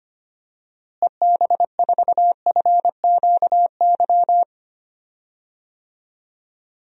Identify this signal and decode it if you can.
Morse code, heard as a keyed tone, about 25 wpm, 700 Hz. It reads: EB4FQY
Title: DXCC-100 Call Signs - 6 Characters - Repeated 3x - 25wpm